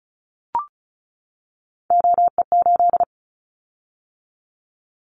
Morse code, heard as a keyed tone, about 34 wpm, 700 Hz. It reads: OE8
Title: Callsign Prefixes for Top 100 DXCC - 35wpm